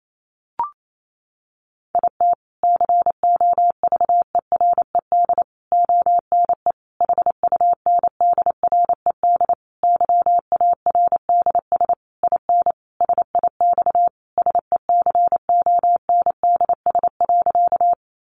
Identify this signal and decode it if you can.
Morse code, heard as a keyed tone, about 28 wpm, 700 Hz. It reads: IT COVERED ONE HUNDRED YARDS IN SIX SECONDS.
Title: A Sound Of Thunder 28wpm